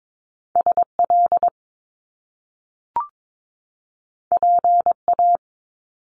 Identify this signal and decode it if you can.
Morse code, heard as a keyed tone, about 22 wpm, 700 Hz. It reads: SL PA
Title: Top Two Letter Combinations in Words 22wpm